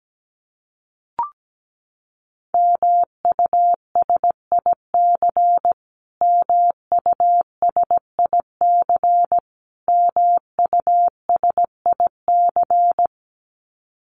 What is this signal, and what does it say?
Morse code, heard as a keyed tone, about 17 wpm, 700 Hz. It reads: MUSIC MUSIC MUSIC
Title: Top 500 Words Repeated 17wpm